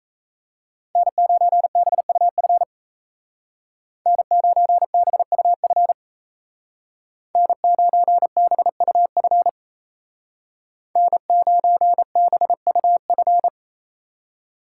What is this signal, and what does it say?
Morse code, heard as a keyed tone, about 35 wpm, 700 Hz. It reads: N9BUF N9BUF N9BUF N9BUF
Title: Speed Racing - USA 1x3 Call Signs 28wpm